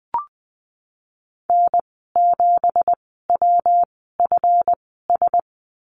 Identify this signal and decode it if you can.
Morse code, heard as a keyed tone, about 20 wpm, spaced as for 15 wpm, 700 Hz. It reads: N7WFS
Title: 200 Lesson 32g - Letter k - Call Signs 20wpm